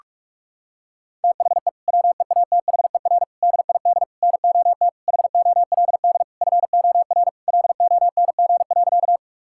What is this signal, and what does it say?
Morse code, heard as a keyed tone, about 41 wpm, 700 Hz. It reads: THE WEATHER DID NOT HOLD FOR LONG.
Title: Sentences from Top 1000 Words - Review Part 5 - 45wpm